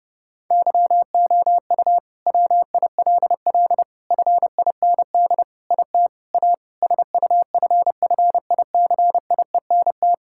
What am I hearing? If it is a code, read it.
Morse code, heard as a keyed tone, about 30 wpm, 700 Hz. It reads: YOU WILL FIND IT A SUFFICIENT
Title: Sentences from Top 1600 Words 30wpm